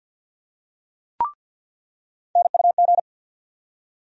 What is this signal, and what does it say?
Morse code, heard as a keyed tone, about 53 wpm, 700 Hz. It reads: NUG
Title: USA Call Sign Postfixes - 3 Letters 50wpm